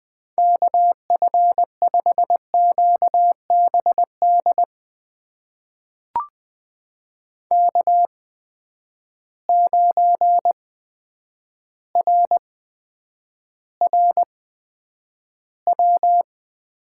Morse code, heard as a keyed tone, about 20 wpm, spaced as for 9 wpm, 700 Hz. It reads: KF5QBD K9RRW
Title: USA Call Signs - 8x Character Spacing - 20wpm